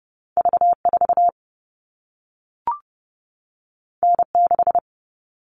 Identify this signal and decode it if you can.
Morse code, heard as a keyed tone, about 30 wpm, 700 Hz. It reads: V 4 N 6
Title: All Cut Numbers - 2-Digit 30wpm